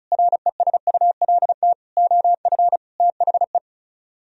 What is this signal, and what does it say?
Morse code, heard as a keyed tone, about 44 wpm, 700 Hz. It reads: RESULT OF THE